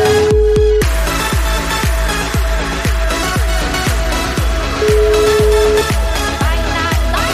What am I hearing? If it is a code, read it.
Ringback tone (Europe, 425 Hz), with noise.